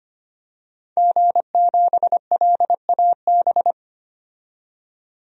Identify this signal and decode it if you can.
Morse code, heard as a keyed tone, about 25 wpm, 700 Hz. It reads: G7LAB
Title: DXCC-100 Call Signs - 5 Characters - 25wpm